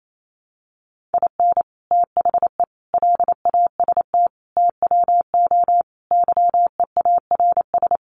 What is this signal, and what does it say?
Morse code, heard as a keyed tone, about 28 wpm, 700 Hz. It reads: IN THE LAST TWO YEARS
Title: Sets of 5 Words 28wpm